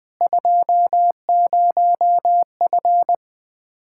Morse code, heard as a keyed tone, about 20 wpm, 700 Hz. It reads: 20F
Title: Practice QSO - Exchange 2 - Info - 20wpm